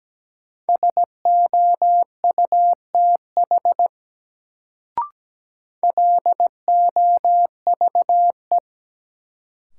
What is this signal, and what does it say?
Morse code, heard as a keyed tone, about 17 wpm, 700 Hz. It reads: SOUTH LOVE
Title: Sentences from Top 400 Words - New Word Drill - 17wpm